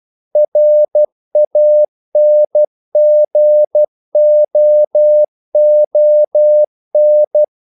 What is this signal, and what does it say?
Morse code, heard as a keyed tone, about 12 wpm, 600 Hz. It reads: RANGOON
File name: CW / Practice / NationalCapitals12